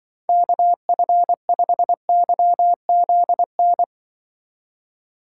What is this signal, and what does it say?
Morse code, heard as a keyed tone, about 23 wpm, 700 Hz. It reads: KF5YZN
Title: Speed Racing - USA 2x3 Call Signs 20wpm